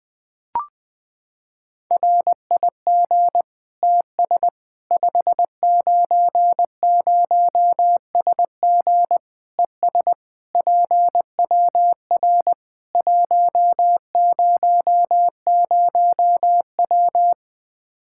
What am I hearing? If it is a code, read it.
Morse code, heard as a keyed tone, about 20 wpm, 700 Hz. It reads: RIG TS 590SG ES PWR 100W
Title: Practice QSO - Exchange 2 - Info - 20wpm